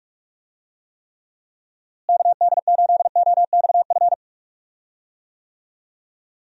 Morse code, heard as a keyed tone, about 42 wpm, 700 Hz. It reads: KD8OXF
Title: USA Call Signs - 6 Characters - Repeated 3x - 45wpm